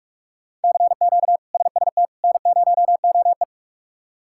Morse code, heard as a keyed tone, about 43 wpm, 700 Hz. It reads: CQ SST N0OE
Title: K1USN SST QSO Practice - 45wpm